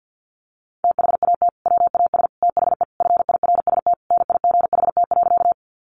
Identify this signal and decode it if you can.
Morse code, heard as a keyed tone, about 50 wpm, 700 Hz. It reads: THAT WAS THE FIRST NIGHT.
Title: Sentences from Top 300 Words 50wpm